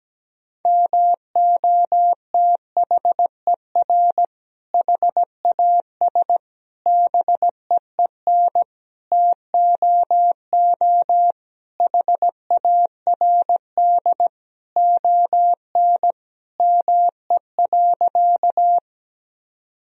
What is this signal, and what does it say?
Morse code, heard as a keyed tone, about 17 wpm, 700 Hz. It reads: MOTHER HAS BEEN TOO HARD ON ME.
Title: Sentences from Top 1000 Words - Review Part 1 - 17wpm